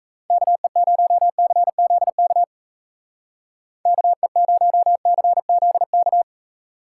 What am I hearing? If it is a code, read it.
Morse code, heard as a keyed tone, about 37 wpm, 700 Hz. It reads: KE0CZK KE0CZK